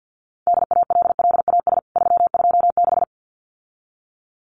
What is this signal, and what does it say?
Morse code, heard as a keyed tone, about 50 wpm, 700 Hz. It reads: DALLAS 326
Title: CWT Exchange 50wpm